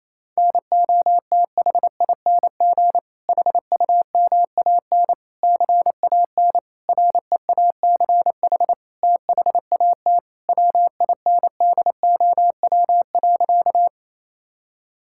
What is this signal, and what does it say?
Morse code, heard as a keyed tone, about 28 wpm, 700 Hz. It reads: NOTHING HUMAN CAN REACH THAT WINDOW.
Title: Sentences from Top 1600 Words 28wpm